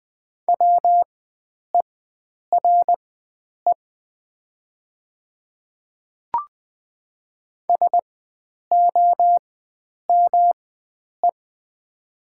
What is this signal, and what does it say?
Morse code, heard as a keyed tone, about 20 wpm, spaced as for 9 wpm, 700 Hz. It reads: WERE SOME
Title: 4x Character Spacing - Top 300 Words - 20wpm